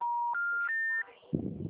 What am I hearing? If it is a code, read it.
Special information tone, with noise.